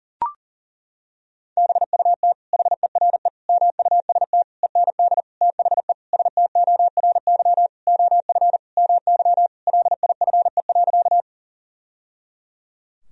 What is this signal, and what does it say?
Morse code, heard as a keyed tone, about 40 wpm, 700 Hz. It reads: BUT HERE MUST END THE STORY OF MY LIFE.
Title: Sentences from Top 1000 Words - Review Part 1 - 40wpm